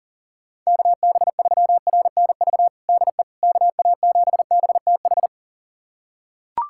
Morse code, heard as a keyed tone, about 40 wpm, 700 Hz. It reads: KB3RNV DE KA7BTH
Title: USA Callsign DE Callsign 40wpm